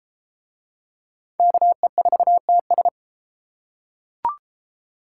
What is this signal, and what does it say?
Morse code, heard as a keyed tone, about 29 wpm, 700 Hz. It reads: KE4TS